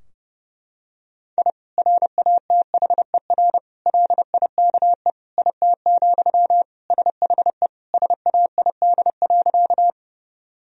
Morse code, heard as a keyed tone, about 32 wpm, 700 Hz. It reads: I RATHER LIKE IT, SHE SAID.
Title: Sentences from Top 1000 Words - Review Part 4 - 30wpm